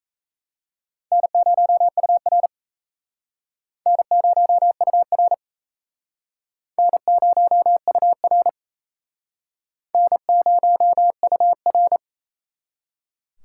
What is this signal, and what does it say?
Morse code, heard as a keyed tone, about 36 wpm, 700 Hz. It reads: N0UR N0UR N0UR N0UR